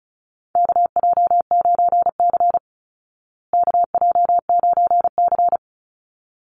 Morse code, heard as a keyed tone, about 34 wpm, 700 Hz. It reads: KJ9C KJ9C
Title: Call Signs Repeated 3x - 35wpm